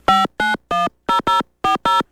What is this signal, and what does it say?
DTMF, heard as telephone touch tones, tones of about 160 ms. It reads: 5910040